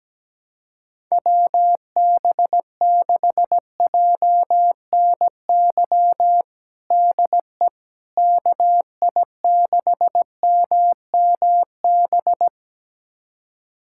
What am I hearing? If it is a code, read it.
Morse code, heard as a keyed tone, about 17 wpm, 700 Hz. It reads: WB6JNY DE KI6MMB
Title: USA Callsign DE Callsign 17wpm